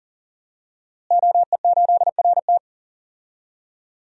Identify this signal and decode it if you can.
Morse code, heard as a keyed tone, about 37 wpm, 700 Hz. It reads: OE8RT
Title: DXCC-100 Call Signs - 5 Characters - Repeated 3x - 40wpm